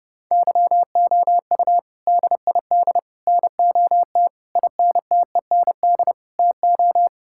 Morse code, heard as a keyed tone, about 30 wpm, 700 Hz. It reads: YOU DID NOT INTEND TO
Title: Sentences from Top 1600 Words 30wpm